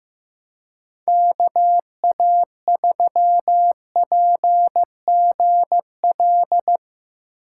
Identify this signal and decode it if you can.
Morse code, heard as a keyed tone, about 16 wpm, 700 Hz. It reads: KA3PGL
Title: USA Call Signs - 6 Characters - 2x Character Spacing - 15wpm